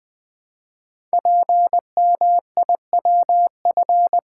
Morse code, heard as a keyed tone, about 20 wpm, 700 Hz. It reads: PMIWF
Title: Code Groups - 5 Letter:Numbers - 20wpm